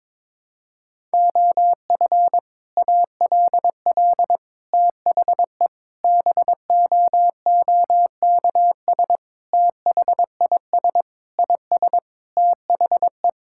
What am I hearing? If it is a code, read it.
Morse code, heard as a keyed tone, about 22 wpm, 700 Hz. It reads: OF ALL THE BOOKS THIS IS THE
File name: practice / 22wpm / Sentences with 3-Word Phrases - 22wpm